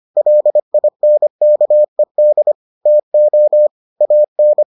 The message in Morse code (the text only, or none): LINKED TO AN